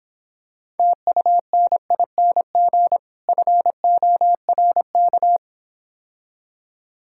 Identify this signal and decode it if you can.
Morse code, heard as a keyed tone, about 26 wpm, 700 Hz. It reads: TUNING FORK